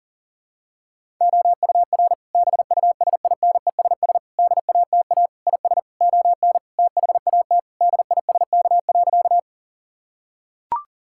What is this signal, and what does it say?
Morse code, heard as a keyed tone, about 40 wpm, 700 Hz. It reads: OUR BUSINESS DATA IS ON THAT DISK.
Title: Sentences from Top 1000 Words - Review Part 6 - 40wpm